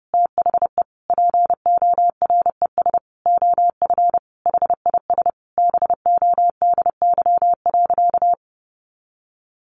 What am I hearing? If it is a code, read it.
Morse code, heard as a keyed tone, about 30 wpm, 700 Hz. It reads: THE PORES OF HIS BODY.
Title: A Sound Of Thunder 30wpm